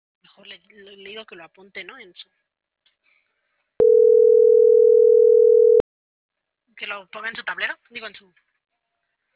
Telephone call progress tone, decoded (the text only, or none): ringback tone